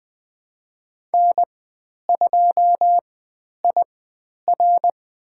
Morse code, heard as a keyed tone, about 20 wpm, spaced as for 10 wpm, 700 Hz. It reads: N2IR
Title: 095 Lesson 16g - Letter c - Call Signs 20wpm